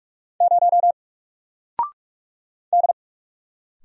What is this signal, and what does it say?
Morse code, heard as a keyed tone, about 46 wpm, 700 Hz. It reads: 0 D
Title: Single Letter-Number 45wpm